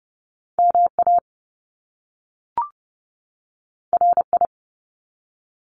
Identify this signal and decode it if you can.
Morse code, heard as a keyed tone, about 30 wpm, 700 Hz. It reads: MA RI
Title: US States 30wpm